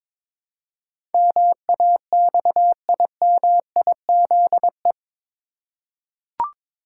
Morse code, heard as a keyed tone, about 21 wpm, 700 Hz. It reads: MAXIMIZE
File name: practice / 22wpm / Eight Letter Words - 22wpm